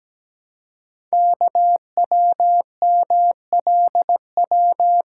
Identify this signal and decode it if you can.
Morse code, heard as a keyed tone, about 17 wpm, 700 Hz. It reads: KWMLW
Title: Code Groups - 5 Letters - 17wpm